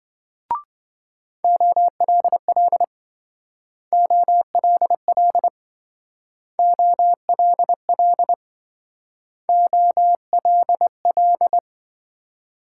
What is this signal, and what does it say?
Morse code, heard as a keyed tone, about 27 wpm, 700 Hz. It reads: OLL OLL OLL OLL